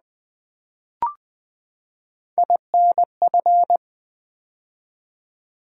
Morse code, heard as a keyed tone, about 20 wpm, 700 Hz. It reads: INF
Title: Build A Word - 3 Letters At A Time - 9-Letter Words - 20wpm